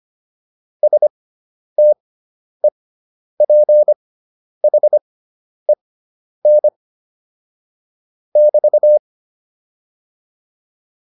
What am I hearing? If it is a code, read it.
Morse code, heard as a keyed tone, about 25 wpm, spaced as for 10 wpm, 600 Hz. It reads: STEPHEN <BT>
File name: QSO107 10